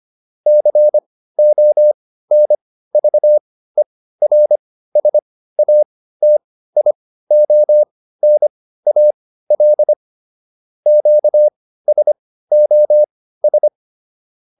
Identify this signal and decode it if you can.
Morse code, heard as a keyed tone, about 25 wpm, spaced as for 15 wpm, 600 Hz. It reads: CONVERSATIONAL QSOS